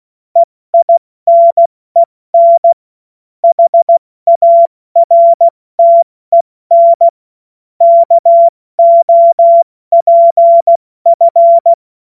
Morse code, heard as a keyed tone, about 16 wpm, 680 Hz. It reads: EINEN HARTEN KOPF
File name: Morsekurs / Witze / Witz 07. März 2023 70 F80